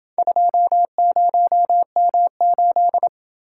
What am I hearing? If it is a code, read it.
Morse code, heard as a keyed tone, about 27 wpm, 700 Hz. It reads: 20M8